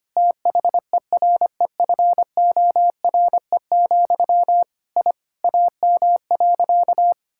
Morse code, heard as a keyed tone, about 25 wpm, 700 Hz. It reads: THEREFORE, I AM.
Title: Sentences from Top 1000 Words - Review Part 5 - 25wpm